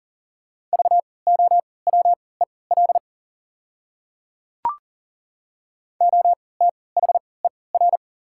Morse code, heard as a keyed tone, about 37 wpm, spaced as for 22 wpm, 690 Hz. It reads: VOWEL OTHER
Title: 3x Character Spacing - Top 500 Words - 40wpm